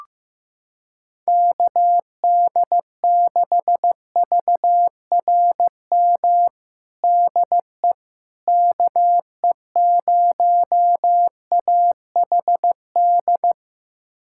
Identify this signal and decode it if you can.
Morse code, heard as a keyed tone, about 15 wpm, 700 Hz. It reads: KD6VRM DE KE0AHD
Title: USA Callsign DE Callsign 15wpm